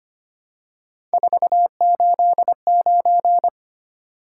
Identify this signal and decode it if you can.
Morse code, heard as a keyed tone, about 25 wpm, 700 Hz. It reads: 489
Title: Top 180 QSO Elements 25wpm